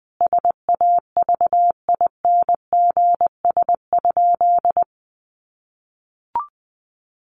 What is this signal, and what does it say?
Morse code, heard as a keyed tone, about 20 wpm, 700 Hz. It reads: SAVINGS?
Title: Sentences from Top 1200 Words 20wpm